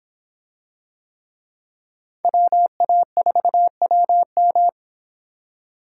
Morse code, heard as a keyed tone, about 24 wpm, 700 Hz. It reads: WA4WM